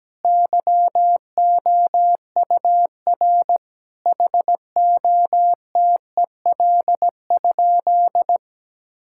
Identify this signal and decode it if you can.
Morse code, heard as a keyed tone, about 17 wpm, 700 Hz. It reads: YOUR HOTEL?
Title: Sentences from Top 1600 Words 17wpm